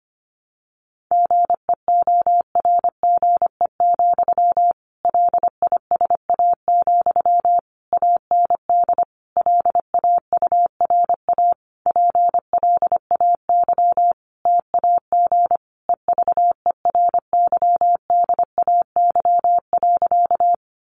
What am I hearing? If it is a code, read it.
Morse code, heard as a keyed tone, about 25 wpm, 700 Hz. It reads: GEORGE, LISA, AND LAURA PLAY TAG EVERYDAY.
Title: Additional Easy Phrases 25wpm